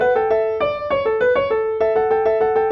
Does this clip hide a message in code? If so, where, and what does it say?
No.